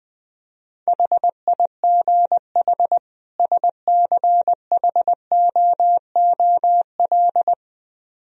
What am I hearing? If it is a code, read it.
Morse code, heard as a keyed tone, about 20 wpm, 700 Hz. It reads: HIGH SCHOOL
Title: Speed-Racing - Sets of 2 Words 17wpm